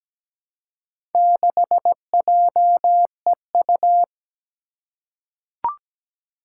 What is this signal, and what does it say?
Morse code, heard as a keyed tone, about 17 wpm, 700 Hz. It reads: 6JEU
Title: Four Challenging Letter-Number Combinations 17wpm